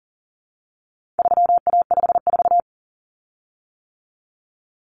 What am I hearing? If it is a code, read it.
Morse code, heard as a keyed tone, about 40 wpm, 700 Hz. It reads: 3A54
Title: Common Cut Numbers - 4-Digit 40wpm